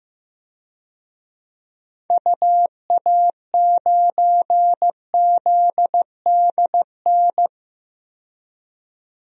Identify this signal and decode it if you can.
Morse code, heard as a keyed tone, about 15 wpm, 700 Hz. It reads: UA9ZDN
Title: DXCC-100 Call Signs - 6 Characters - 15wpm